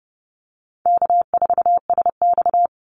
Morse code, heard as a keyed tone, about 31 wpm, 700 Hz. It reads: K4SX
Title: Easy Call Signs - 30wpm